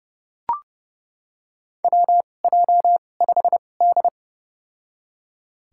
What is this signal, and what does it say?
Morse code, heard as a keyed tone, about 30 wpm, spaced as for 22 wpm, 700 Hz. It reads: WJ5D
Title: USA Call Signs - 4 Characters - 2x Character Spacing - 30wpm